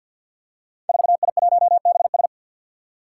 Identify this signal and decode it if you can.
Morse code, heard as a keyed tone, about 47 wpm, 700 Hz. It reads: 4I1BS
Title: Five Challenging Letter-Number Combinations 50wpm